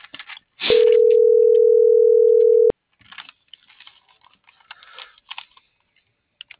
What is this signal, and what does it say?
Ringback tone (North America), 440 + 480 Hz.